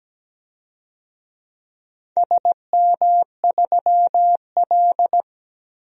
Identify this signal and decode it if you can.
Morse code, heard as a keyed tone, about 17 wpm, 700 Hz. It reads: SM3L
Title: DXCC-100 Call Signs - 4 Characters - 17wpm